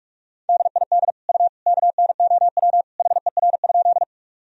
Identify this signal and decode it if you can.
Morse code, heard as a keyed tone, about 46 wpm, 700 Hz. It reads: DID U KNOW HER?
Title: Sentences from Top 100 Words and Abbreviations 45wpm